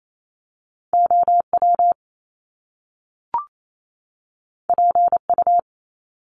Morse code, heard as a keyed tone, about 28 wpm, 700 Hz. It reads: OW PU
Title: Top Two Letter Combinations in Words 28wpm